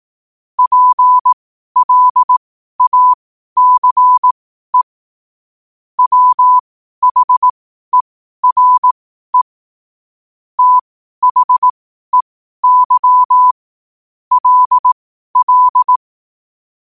Morse code, heard as a keyed tone, about 18 wpm, spaced as for 12 wpm, 1000 Hz. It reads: PLACE WHERE THEY LL
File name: qotdfiles / 12wpm